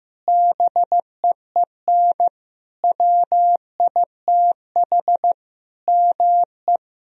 Morse code, heard as a keyed tone, about 16 wpm, 700 Hz. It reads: BEEN WITH ME
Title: Sentences from Top 1000 Words - Review Part 4 - 15wpm